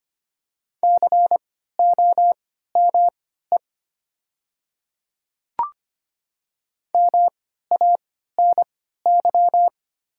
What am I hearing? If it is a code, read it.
Morse code, heard as a keyed tone, about 25 wpm, spaced as for 14 wpm, 700 Hz. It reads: COME MANY
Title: 3x Character Spacing - Top 100 Words - 25wpm